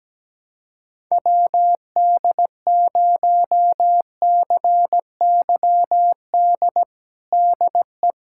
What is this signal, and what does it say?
Morse code, heard as a keyed tone, about 17 wpm, 700 Hz. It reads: WD0CYD DE
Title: USA Callsign DE Callsign - 6 Characters - 17wpm